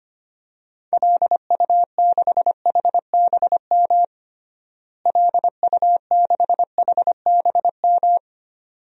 Morse code, heard as a keyed tone, about 25 wpm, 700 Hz. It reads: LU6HBM LU6HBM
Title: DXCC-100 Call Signs - 6 Characters - Repeated 2x - 25wpm